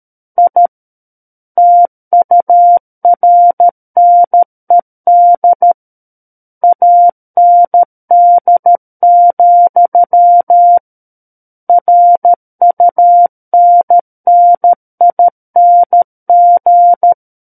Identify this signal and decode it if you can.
Morse code, heard as a keyed tone, about 13 wpm, 700 Hz. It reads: I TURNED AND, RUNNING
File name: Book One / 04-Chapter 4 - 13 WPM